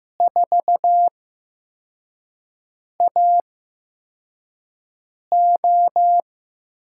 Morse code, heard as a keyed tone, about 15 wpm, spaced as for 4 wpm, 700 Hz. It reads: 4AO
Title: USA Call Signs - 5 Characters - 8x Character Spacing - 15wpm